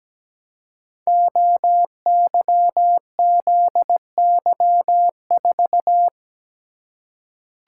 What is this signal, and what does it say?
Morse code, heard as a keyed tone, about 17 wpm, 700 Hz. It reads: OYZY4
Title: Code Groups - 5 Letter:Numbers - 17wpm